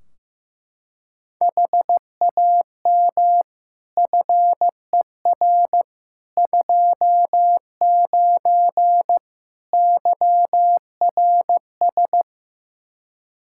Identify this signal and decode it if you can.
Morse code, heard as a keyed tone, about 15 wpm, 700 Hz. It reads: HAM FER 29 YRS
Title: Practice QSO - Exchange 3 - More Info - 15wpm